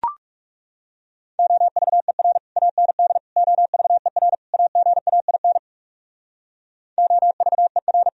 Morse code, heard as a keyed tone, about 40 wpm, 700 Hz. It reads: OVER AND OVER AGAIN OVER